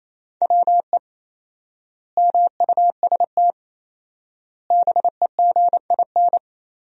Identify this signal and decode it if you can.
Morse code, heard as a keyed tone, about 28 wpm, 700 Hz. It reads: WE MUST BEGIN
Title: 4x Word-Spacing - Sentences from Top 300 Words - 28wpm